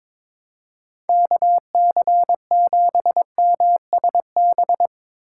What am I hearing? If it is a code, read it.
Morse code, heard as a keyed tone, about 24 wpm, 700 Hz. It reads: KC7MSB
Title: Speed Racing - USA 2x3 Call Signs 22wpm